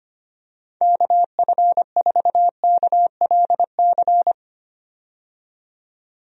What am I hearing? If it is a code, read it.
Morse code, heard as a keyed tone, about 25 wpm, 700 Hz. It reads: KF4KLC
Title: USA Call Signs - 6 Characters - 25wpm